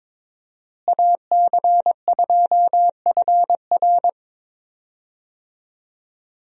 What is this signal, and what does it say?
Morse code, heard as a keyed tone, about 22 wpm, 700 Hz. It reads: AC2FR
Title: USA Call Signs - 5 Characters - Repeated 3x - 22wpm